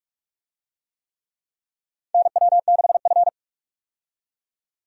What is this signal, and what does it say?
Morse code, heard as a keyed tone, about 46 wpm, 700 Hz. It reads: NW6F
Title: USA Call Signs - 4 Characters - 45wpm